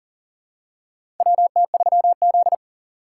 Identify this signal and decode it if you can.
Morse code, heard as a keyed tone, about 38 wpm, 700 Hz. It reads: WT3Z
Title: USA Call Signs - 4 Characters - 3x Character Spacing - 40wpm